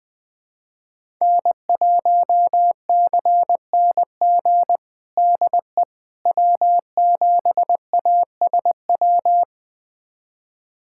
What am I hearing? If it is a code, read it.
Morse code, heard as a keyed tone, about 20 wpm, 700 Hz. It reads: N1CNG DE W7ASW
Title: USA Callsign DE Callsign - 5 Characters - 20wpm